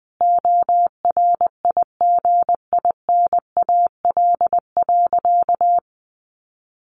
Morse code, heard as a keyed tone, about 20 wpm, 700 Hz. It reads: ORIGINAL.
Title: Sentences from Top 1500 Words 20wpm